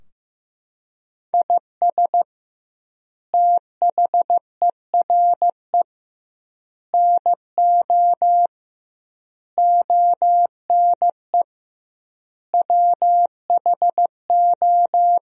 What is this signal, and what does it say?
Morse code, heard as a keyed tone, about 15 wpm, 700 Hz. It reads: IS THERE NO ONE WHO
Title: 2x Word-Spacing - Sentences from Top 200 Words - 15wpm